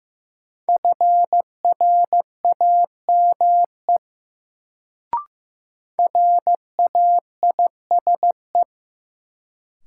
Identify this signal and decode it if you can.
Morse code, heard as a keyed tone, about 15 wpm, 700 Hz. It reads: FRAME RAISE
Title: Sentences from Top 1400 Words - New Word Drill Alt - 15wpm